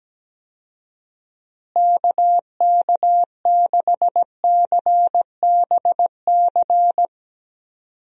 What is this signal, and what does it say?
Morse code, heard as a keyed tone, about 17 wpm, 700 Hz. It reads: KK6CBC
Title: USA Call Signs - 6 Characters - Repeated 3x - 17wpm